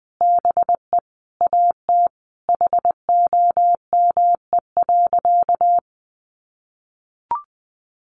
Morse code, heard as a keyed tone, about 20 wpm, 700 Hz. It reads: BE AT HOME.
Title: Sentences from Top 1500 Words 20wpm